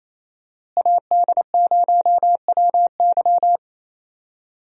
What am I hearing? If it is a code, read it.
Morse code, heard as a keyed tone, about 27 wpm, 700 Hz. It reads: AD0WY
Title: USA Call Signs - 5 Characters - Repeated 3x - 28wpm